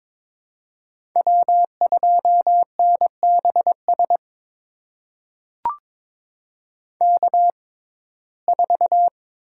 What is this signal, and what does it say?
Morse code, heard as a keyed tone, about 22 wpm, spaced as for 16 wpm, 700 Hz. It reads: W2NBS K4
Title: USA Call Signs - 5 Characters - 6x Character Spacing - 22wpm